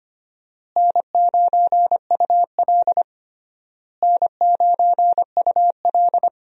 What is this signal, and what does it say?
Morse code, heard as a keyed tone, about 25 wpm, 700 Hz. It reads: N9UL N9UL
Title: USA Call Signs - 4 Characters - Repeated 3x - 25wpm